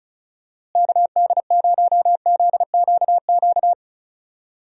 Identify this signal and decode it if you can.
Morse code, heard as a keyed tone, about 36 wpm, 700 Hz. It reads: KD0ZQQ
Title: USA Call Signs - 6 Characters - Repeated 2x - 35wpm